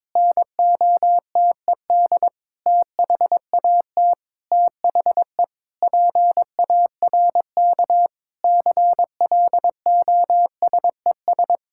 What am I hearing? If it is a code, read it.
Morse code, heard as a keyed tone, about 22 wpm, 700 Hz. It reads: NOTED THAT THE PARK CLOSES
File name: Sentences with 7-Word Phrases - 22wpm